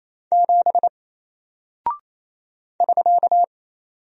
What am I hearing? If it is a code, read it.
Morse code, heard as a keyed tone, about 27 wpm, 700 Hz. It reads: 7 <SK>
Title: Single Character (Letter:Number:Punctuation:Prosign) 28wpm